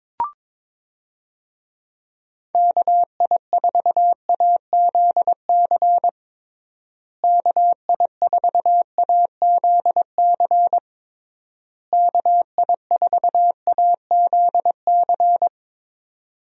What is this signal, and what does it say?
Morse code, heard as a keyed tone, about 21 wpm, 700 Hz. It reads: KI4AZC KI4AZC KI4AZC
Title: USA Call Signs - 6 Characters - Repeated 3x - 22wpm